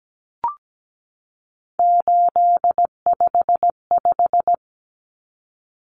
Morse code, heard as a keyed tone, about 17 wpm, 700 Hz. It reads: 855
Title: Common Cut Numbers - 3-Digit 17wpm